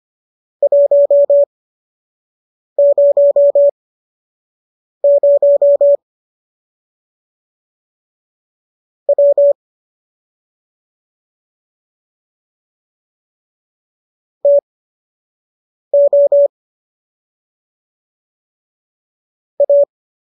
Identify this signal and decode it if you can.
Morse code, heard as a keyed tone, about 25 wpm, spaced as for 6 wpm, 570 Hz. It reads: 100 W TO A